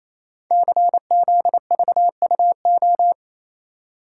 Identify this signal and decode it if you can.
Morse code, heard as a keyed tone, about 29 wpm, 700 Hz. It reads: CZVUO
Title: Code Groups - 5 Letter:Numbers - 28wpm